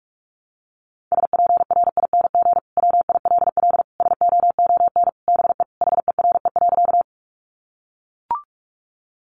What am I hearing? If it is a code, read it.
Morse code, heard as a keyed tone, about 44 wpm, 700 Hz. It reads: SPRING WILL SOON BE HERE.
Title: Sentences from Top 1200 Words 45wpm